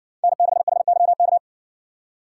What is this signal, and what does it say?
Morse code, heard as a keyed tone, about 59 wpm, 700 Hz. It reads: N6HYK